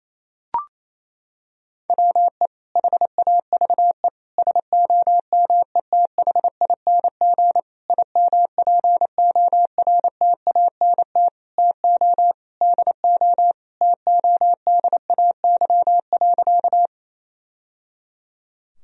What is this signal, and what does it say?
Morse code, heard as a keyed tone, about 28 wpm, 700 Hz. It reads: WE HAVE SOMETHING IMPORTANT TO DO TODAY.
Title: Sentences from Top 1000 Words - Review Part 4 - 28wpm